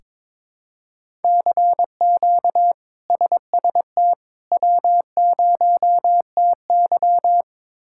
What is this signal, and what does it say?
Morse code, heard as a keyed tone, about 22 wpm, 700 Hz. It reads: CQ SST W0TY
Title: K1USN SST QSO Practice - 22wpm